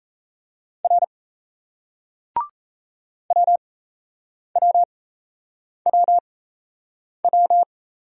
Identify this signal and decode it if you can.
Morse code, heard as a keyed tone, about 37 wpm, spaced as for 8 wpm, 700 Hz. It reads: R WWWW